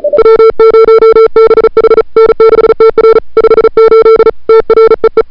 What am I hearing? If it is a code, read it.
Morse code, heard as a keyed tone, about 35 wpm, 440 Hz. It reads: W0BH N6TR 58 TREE
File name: CWSprint / February 2007 CW Sprint / 01-2007 CW Sprint Hour1